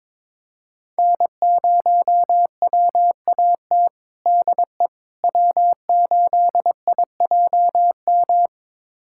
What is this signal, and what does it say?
Morse code, heard as a keyed tone, about 22 wpm, 700 Hz. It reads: N0WAT DE W8IJM
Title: USA Callsign DE Callsign - 5 Characters - 22wpm